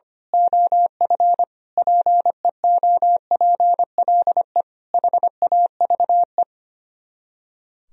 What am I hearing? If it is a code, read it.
Morse code, heard as a keyed tone, about 24 wpm, 700 Hz. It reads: OF PEOPLE HAVE
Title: Speed-Racing - Sets of 5 Words 25wpm